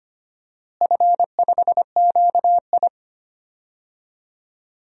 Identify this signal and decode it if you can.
Morse code, heard as a keyed tone, about 25 wpm, 700 Hz. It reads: F5QI
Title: DXCC-100 Call Signs - 4 Characters - Repeated 3x - 25wpm